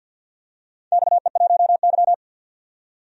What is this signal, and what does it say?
Morse code, heard as a keyed tone, about 52 wpm, 700 Hz. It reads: XE1Y
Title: DXCC-100 Call Signs - 4 Characters - 50wpm